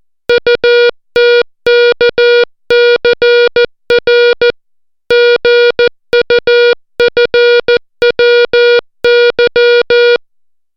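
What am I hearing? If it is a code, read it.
Morse code, heard as a keyed tone, about 14 wpm, 480 Hz. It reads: UTKCR GUFWY
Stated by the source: Morse alphabet lesson, 60 characters per minute.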